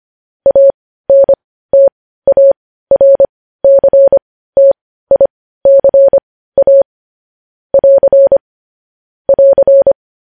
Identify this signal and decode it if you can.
Morse code, heard as a keyed tone, about 25 wpm, spaced as for 15 wpm, 550 Hz. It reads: ANTARCTICA <AR> <AR>
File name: podcast / News headlines-1x-25x15